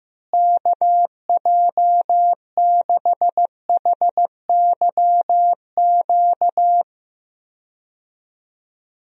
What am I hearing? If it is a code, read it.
Morse code, heard as a keyed tone, about 15 wpm, 700 Hz. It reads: KJ6HYQ
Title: USA Call Signs - 6 Characters - Repeated 2x - 15wpm